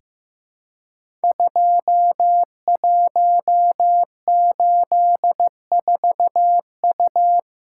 Morse code, heard as a keyed tone, about 15 wpm, 700 Hz. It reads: 2184U
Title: Code Groups - 5 Letter:Numbers - 15wpm